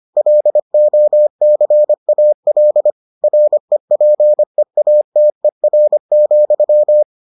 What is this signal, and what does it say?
Morse code, heard as a keyed tone, about 25 wpm, 600 Hz. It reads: LOCAL REPEATER,